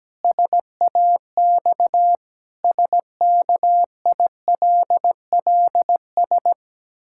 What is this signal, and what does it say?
Morse code, heard as a keyed tone, about 17 wpm, 700 Hz. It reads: SAX SKILLS
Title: Pangram Challenge v2 - 17wpm